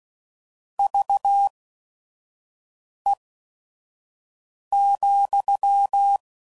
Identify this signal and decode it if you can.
Morse code, heard as a keyed tone, about 16 wpm, spaced as for 5 wpm, 800 Hz. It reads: VE,
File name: CW Code Groups @5WPM / QSOs / kc7sac